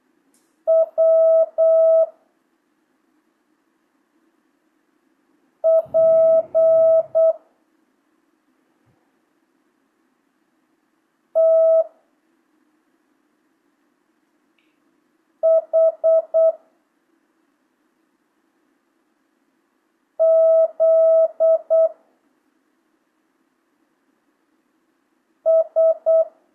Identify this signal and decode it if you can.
Morse code, heard as a keyed tone, about 8 wpm, 650 Hz. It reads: W P T H Z S
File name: jh8cbh / 2025cw